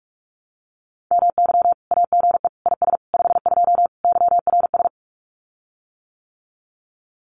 Morse code, heard as a keyed tone, about 44 wpm, 700 Hz. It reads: MY AGE IS 52 YRS
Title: Common QSO Phrases Encore 45wpm